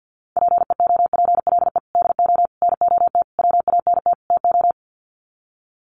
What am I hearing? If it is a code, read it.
Morse code, heard as a keyed tone, about 50 wpm, 700 Hz. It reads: PEOPLE DO NOT WANT TO
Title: Sets of 5 Words Encore2 50wpm